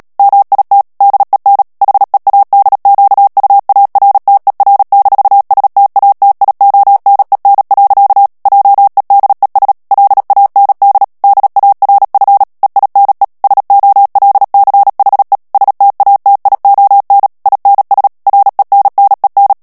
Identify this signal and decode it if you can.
Morse code, heard as a keyed tone, about 37 wpm, 800 Hz. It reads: MIT DEN HEADQUARTER-STATIONEN. JEDES LAND DARF EINE SOLCHE STATION INS RENNEN
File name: morse mp3 / ALLEMAND texte clair 2 vitesse 35 mots minute